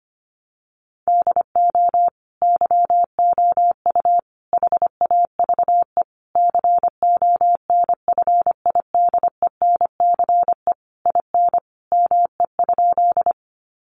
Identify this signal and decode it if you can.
Morse code, heard as a keyed tone, about 25 wpm, 700 Hz. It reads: DO YOU HAVE CONFIDENCE IN ME?
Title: Sentences from Top 1500 Words 25wpm